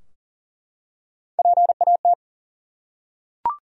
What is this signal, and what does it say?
Morse code, heard as a keyed tone, about 40 wpm, 700 Hz. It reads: PAT
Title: USA Common Names - Max 5 Letters - 40wpm